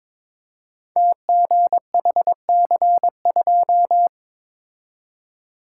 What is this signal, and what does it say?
Morse code, heard as a keyed tone, about 22 wpm, 700 Hz. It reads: TGHC2
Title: Code Groups - 5 Letter:Numbers - 22wpm